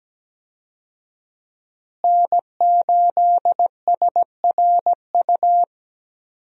Morse code, heard as a keyed tone, about 17 wpm, 700 Hz. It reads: N8SRU
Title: USA Call Signs - 5 Characters - Repeated 2x - 17wpm